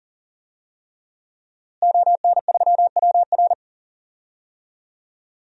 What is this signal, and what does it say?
Morse code, heard as a keyed tone, about 37 wpm, 700 Hz. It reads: ON3WR
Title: DXCC-100 Call Signs - 5 Characters - Repeated 3x - 40wpm